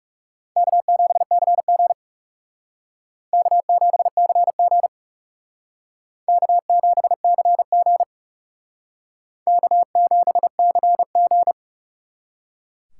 Morse code, heard as a keyed tone, about 41 wpm, 700 Hz. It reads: K7CG K7CG K7CG K7CG